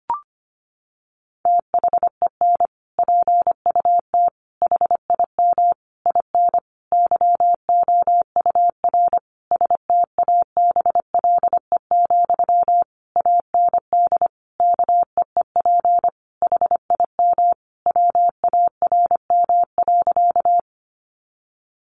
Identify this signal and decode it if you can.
Morse code, heard as a keyed tone, about 25 wpm, 700 Hz. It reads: THEN PUT HIM IN YOUR STABLE, AND KEEP HIM WARM.
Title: Sentences from Top 1500 Words 25wpm